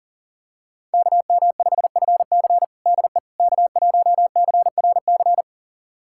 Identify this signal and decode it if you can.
Morse code, heard as a keyed tone, about 40 wpm, 700 Hz. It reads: KM5FC DE K1CRC
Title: USA Callsign DE Callsign - 5 Characters - 40wpm